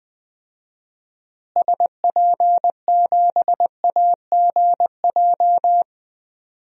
Morse code, heard as a keyed tone, about 20 wpm, 700 Hz. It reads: SP7AGJ
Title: DXCC-100 Call Signs - 6 Characters - Repeated 2x - 20wpm